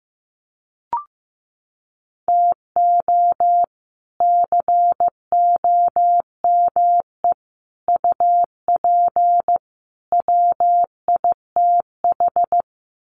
Morse code, heard as a keyed tone, about 15 wpm, 700 Hz. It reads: TO COME UP WITH
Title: Sets of 4 Words 15wpm